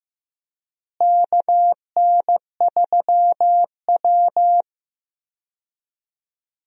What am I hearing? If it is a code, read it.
Morse code, heard as a keyed tone, about 15 wpm, 700 Hz. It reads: KN3W
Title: USA Call Signs - 4 Characters - Repeated 2x - 15wpm